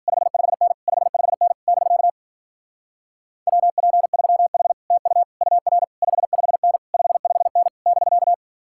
Morse code, heard as a keyed tone, about 49 wpm, 700 Hz. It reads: 55N 55N <BK> WP3H TU UR 55N 55N <BK>
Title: Speed-Racing Common QSO Phrases - 40wpm